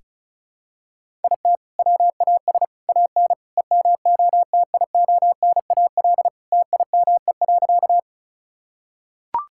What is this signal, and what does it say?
Morse code, heard as a keyed tone, about 36 wpm, 690 Hz. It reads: IT WAS AN EMOTIONAL TIME.
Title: Sentences from Top 1000 Words - Review Part 5 - 35wpm